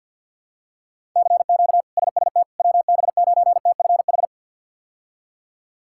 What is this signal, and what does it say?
Morse code, heard as a keyed tone, about 47 wpm, 700 Hz. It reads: CQ SST WB9TFH
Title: K1USN SST QSO Practice - 50wpm